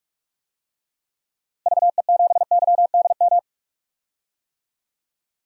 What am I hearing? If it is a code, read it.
Morse code, heard as a keyed tone, about 41 wpm, 700 Hz. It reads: VE7YDM
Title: DXCC-100 Call Signs - 6 Characters - Repeated 2x - 45wpm